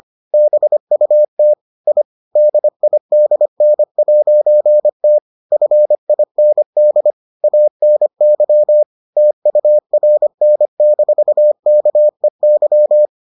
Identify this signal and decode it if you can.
Morse code, heard as a keyed tone, about 25 wpm, 600 Hz. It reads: BUT I DIDN'T FIND ANY TURN-KEY